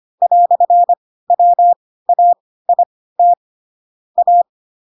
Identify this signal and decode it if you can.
Morse code, heard as a keyed tone, about 25 wpm, spaced as for 16 wpm, 700 Hz. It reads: "WAIT A